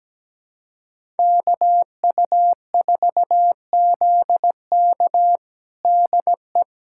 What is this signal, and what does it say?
Morse code, heard as a keyed tone, about 17 wpm, 700 Hz. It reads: KU4ZK DE
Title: USA Callsign DE Callsign - 5 Characters - 17wpm